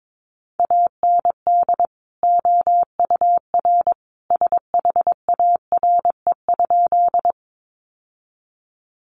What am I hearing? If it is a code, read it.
Morse code, heard as a keyed tone, about 22 wpm, 700 Hz. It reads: AND OUR SHARE?
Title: Sentences from Top 900 Words 22wpm